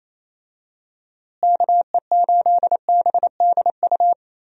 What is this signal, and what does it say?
Morse code, heard as a keyed tone, about 28 wpm, 700 Hz. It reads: KE8BDU
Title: USA Call Signs - 6 Characters - Repeated 3x - 28wpm